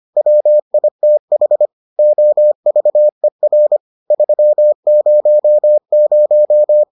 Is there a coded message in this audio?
Yes: Morse code from 0.2 to 6.8 s, heard as a keyed tone, about 25 wpm, 600 Hz. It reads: WITH OVER 300